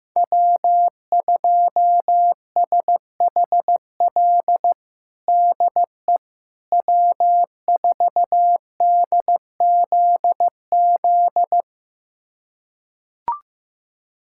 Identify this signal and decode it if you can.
Morse code, heard as a keyed tone, about 15 wpm, 700 Hz. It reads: W2SHL DE W4DZZ
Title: USA Callsign DE Callsign - 5 Characters - 15wpm